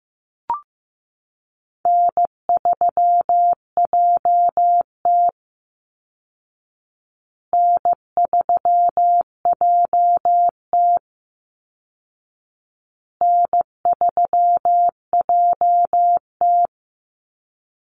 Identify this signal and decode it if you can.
Morse code, heard as a keyed tone, about 15 wpm, 700 Hz. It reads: N3JT N3JT N3JT
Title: Call Signs Repeated 2x - 15wpm